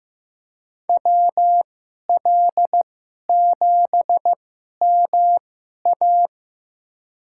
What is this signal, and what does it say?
Morse code, heard as a keyed tone, about 15 wpm, spaced as for 11 wpm, 700 Hz. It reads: WL7MA
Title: USA Call Signs - 5 Characters - 2x Character Spacing - 15wpm